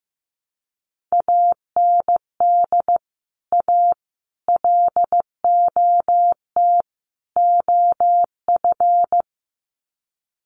Morse code, heard as a keyed tone, about 15 wpm, 700 Hz. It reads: AND A LOT OF
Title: Sets of 4 Words 15wpm